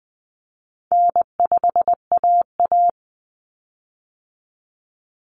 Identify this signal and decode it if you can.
Morse code, heard as a keyed tone, about 20 wpm, 700 Hz. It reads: N5AA
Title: Common Cut Numbers - 4-Digit 20wpm